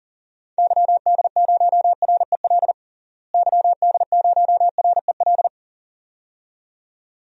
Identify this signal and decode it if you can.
Morse code, heard as a keyed tone, about 40 wpm, 700 Hz. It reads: YD0REL YD0REL
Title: DXCC-100 Call Signs - 6 Characters - Repeated 2x - 40wpm